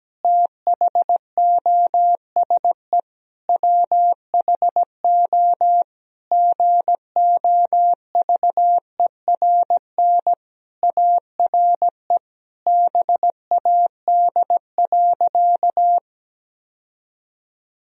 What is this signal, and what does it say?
Morse code, heard as a keyed tone, about 17 wpm, 700 Hz. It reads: THOSE WHO GOVERN ARE BAD.
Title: Sentences from Top 1000 Words - Review Part 2 - 17wpm